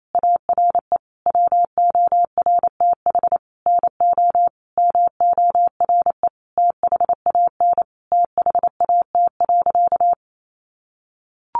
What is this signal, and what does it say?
Morse code, heard as a keyed tone, about 28 wpm, 700 Hz. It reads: ARE WORTH NO MORE THAN THAT.
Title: Sentences from Top 800 Words 28wpm